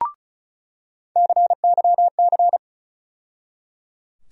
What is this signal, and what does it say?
Morse code, heard as a keyed tone, about 34 wpm, 700 Hz. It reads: CYC
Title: USA Call Sign Postfixes - 3 Letters 35wpm